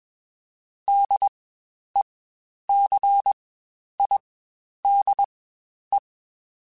Morse code, heard as a keyed tone, about 21 wpm, spaced as for 10 wpm, 800 Hz. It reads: DECIDE
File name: Spaced / Chuckles Spaced-mp3s / HalfFull sp @20